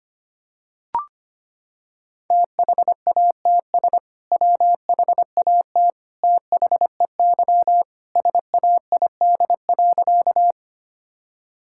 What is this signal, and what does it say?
Morse code, heard as a keyed tone, about 25 wpm, 700 Hz. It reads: THATS WHAT THEY SAID.
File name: practice / 25wpm / Sentences with 2-Word Phrases - 25wpm